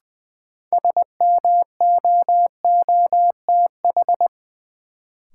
Morse code, heard as a keyed tone, about 20 wpm, 700 Hz. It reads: SMOOTH
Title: Sentences from Top 1400 Words - New Word Drill Alt - 20wpm